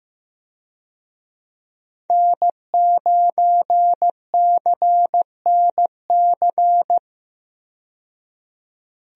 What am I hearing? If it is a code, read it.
Morse code, heard as a keyed tone, about 15 wpm, 700 Hz. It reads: N9CNC
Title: USA Call Signs - 5 Characters - 15wpm